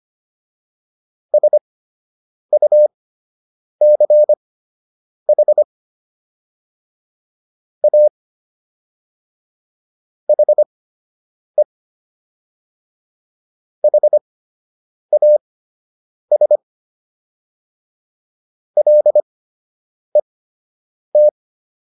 Morse code, heard as a keyed tone, about 25 wpm, spaced as for 8 wpm, 600 Hz. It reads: SUCH A HE HAS LET